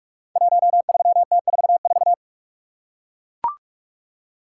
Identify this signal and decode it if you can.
Morse code, heard as a keyed tone, about 44 wpm, 700 Hz. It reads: 13T44
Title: Code Groups - 5 Letter:Numbers - 45wpm